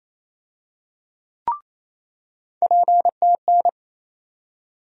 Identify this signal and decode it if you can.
Morse code, heard as a keyed tone, about 28 wpm, 700 Hz. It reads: PTN
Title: USA Call Sign Postfixes - 3 Letters 28wpm